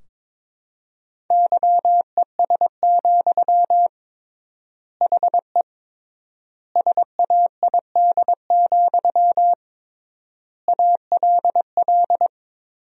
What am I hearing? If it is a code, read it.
Morse code, heard as a keyed tone, about 22 wpm, 700 Hz. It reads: YES, HE SAID, ALL
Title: 3x Word-Spacing - Sentences from Top 500 Words - 22wpm